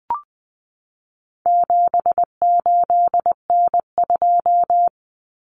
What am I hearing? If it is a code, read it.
Morse code, heard as a keyed tone, about 20 wpm, 700 Hz. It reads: 78N2
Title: Common Cut Numbers - 4-Digit 20wpm